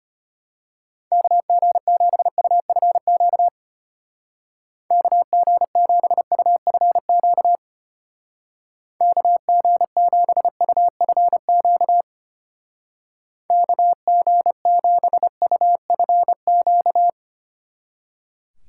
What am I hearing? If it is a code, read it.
Morse code, heard as a keyed tone, about 33 wpm, 700 Hz. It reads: KG7UFQ KG7UFQ KG7UFQ KG7UFQ